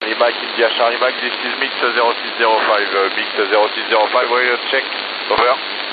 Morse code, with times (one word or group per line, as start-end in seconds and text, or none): none